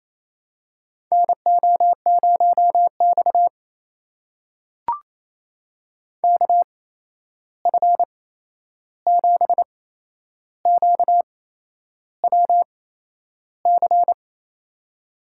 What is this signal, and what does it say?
Morse code, heard as a keyed tone, about 28 wpm, spaced as for 11 wpm, 700 Hz. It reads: NO0X KF7QWC
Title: USA Call Signs - 8x Character Spacing - 28wpm